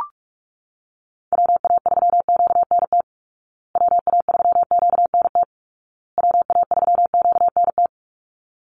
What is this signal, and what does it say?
Morse code, heard as a keyed tone, about 44 wpm, 700 Hz. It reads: WA3QNT WA3QNT WA3QNT
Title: Call Signs Repeated 2x - 45wpm